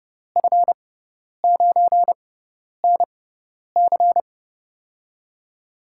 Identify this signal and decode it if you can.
Morse code, heard as a keyed tone, about 32 wpm, spaced as for 10 wpm, 700 Hz. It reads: F9NC
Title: USA Call Signs - 5 Characters - 6x Character Spacing - 30wpm